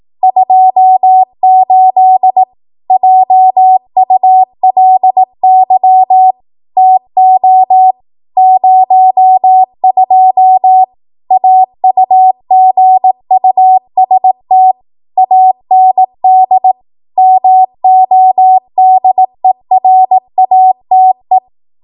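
Morse code, heard as a keyed tone, about 18 wpm, 750 Hz. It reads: 28 JULY TO 02 AUGUST AND MODERATE
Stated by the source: 18 WPM Morse code practice from a national amateur radio association's code archive.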